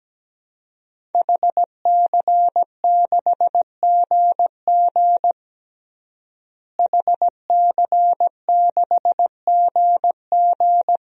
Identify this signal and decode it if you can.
Morse code, heard as a keyed tone, about 17 wpm, 700 Hz. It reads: HC6GG HC6GG
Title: DXCC-100 Call Signs - 5 Characters - Repeated 2x - 17wpm